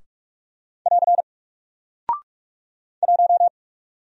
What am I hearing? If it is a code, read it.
Morse code, heard as a keyed tone, about 45 wpm, 700 Hz. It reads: <AR> 1
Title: Single Character (Letter:Number:Punctuation:Prosign) 45wpm